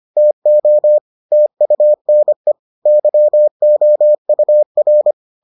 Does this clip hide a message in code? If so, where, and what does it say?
Yes: Morse code from 0.2 to 5.1 s, heard as a keyed tone, about 25 wpm, 600 Hz. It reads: TO TUNE YOUR